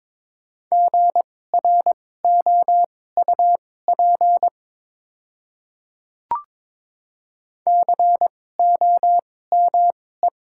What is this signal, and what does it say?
Morse code, heard as a keyed tone, about 22 wpm, spaced as for 16 wpm, 700 Hz. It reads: GROUP COME
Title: 2x Character Spacing - Top 300 Words - 22wpm